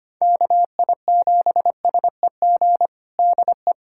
Morse code, heard as a keyed tone, about 25 wpm, 700 Hz. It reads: KI7SEG DE
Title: USA Callsign DE Callsign - 6 Characters - 25wpm